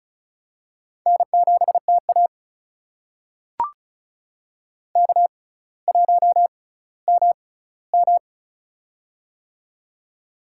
Morse code, heard as a keyed tone, about 36 wpm, spaced as for 18 wpm, 700 Hz. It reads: N7TA K1MM
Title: USA Call Signs - 4 Characters - 6x Character Spacing - 35wpm